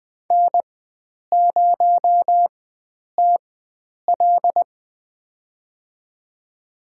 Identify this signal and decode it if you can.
Morse code, heard as a keyed tone, about 20 wpm, spaced as for 9 wpm, 700 Hz. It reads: N0TL
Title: USA Call Signs - 4x Character Spacing - 20wpm